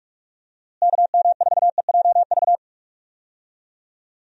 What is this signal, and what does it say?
Morse code, heard as a keyed tone, about 46 wpm, 700 Hz. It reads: KM4EJV
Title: USA Call Signs - 6 Characters - Repeated 2x - 45wpm